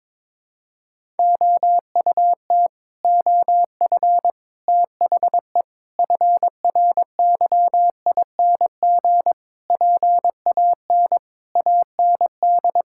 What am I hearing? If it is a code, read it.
Morse code, heard as a keyed tone, about 22 wpm, 700 Hz. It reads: OUT OF THE FRYING PAN AND
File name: Sets of 9 Words - 22wpm